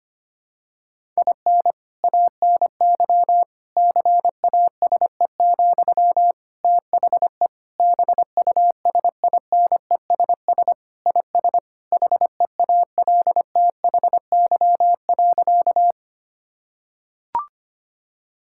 Morse code, heard as a keyed tone, about 25 wpm, 700 Hz. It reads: IN ANY CASE, THE BUSINESS IS HEALTHY.
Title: Sentences from Top 1000 Words - Review Part 5 - 25wpm